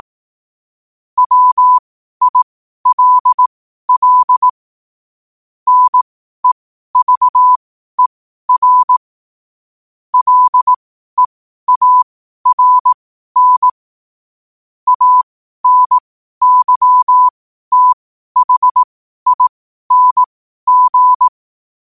Morse code, heard as a keyed tone, about 18 wpm, spaced as for 12 wpm, 1000 Hz. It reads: WILL NEVER LEARN ANYTHING
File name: 12wpm